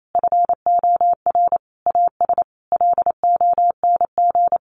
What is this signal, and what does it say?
Morse code, heard as a keyed tone, about 28 wpm, 700 Hz. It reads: FOR AS LONG
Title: Sentences from Top 900 Words 28wpm